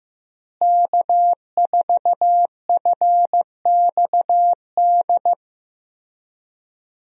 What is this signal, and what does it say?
Morse code, heard as a keyed tone, about 15 wpm, 700 Hz. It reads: K4FXD
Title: USA Call Signs - 5 Characters - Repeated 3x - 15wpm